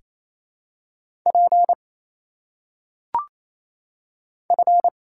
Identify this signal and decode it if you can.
Morse code, heard as a keyed tone, about 27 wpm, 700 Hz. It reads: P F
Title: Single Letters - Phonetic Pronunciation 28wpm